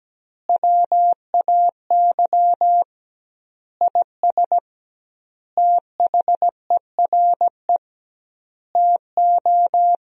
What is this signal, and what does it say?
Morse code, heard as a keyed tone, about 17 wpm, 700 Hz. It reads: WAY IS THERE TO
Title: 2x Word-Spacing - Sentences from Top 100 Words - 17wpm